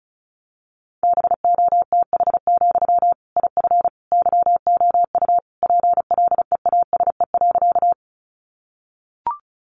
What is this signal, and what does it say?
Morse code, heard as a keyed tone, about 34 wpm, 700 Hz. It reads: BOTH, IF YOU PLEASE.
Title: Sentences from Top 1500 Words 35wpm